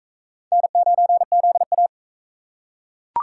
Morse code, heard as a keyed tone, about 36 wpm, 700 Hz. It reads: N9ZA